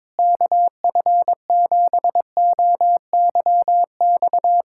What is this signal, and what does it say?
Morse code, heard as a keyed tone, about 21 wpm, 700 Hz. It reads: KF7OYX
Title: USA Callsign DE Callsign - 6 Characters - 22wpm